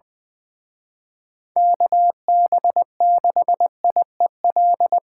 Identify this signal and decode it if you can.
Morse code, heard as a keyed tone, about 24 wpm, 700 Hz. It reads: KB6IEL
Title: Speed Racing - USA 2x3 Call Signs 20wpm